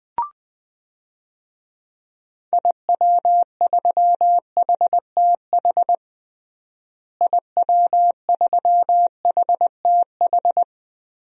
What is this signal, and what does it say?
Morse code, heard as a keyed tone, about 20 wpm, 700 Hz. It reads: IW3HTH IW3HTH
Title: DXCC-100 Call Signs - 6 Characters - Repeated 3x - 20wpm